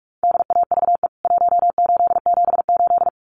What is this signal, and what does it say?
Morse code, heard as a keyed tone, about 44 wpm, 700 Hz. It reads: DAVE 1878
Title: CWT Exchange 45wpm